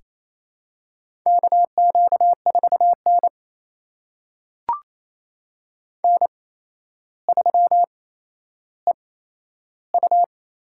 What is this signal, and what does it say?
Morse code, heard as a keyed tone, about 28 wpm, spaced as for 12 wpm, 700 Hz. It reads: KQ4N N3EU
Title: USA Call Signs - 8x Character Spacing - 28wpm